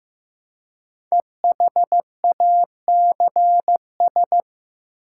Morse code, heard as a keyed tone, about 15 wpm, 700 Hz. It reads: EHACS
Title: Code Groups - 5 Letters - 15wpm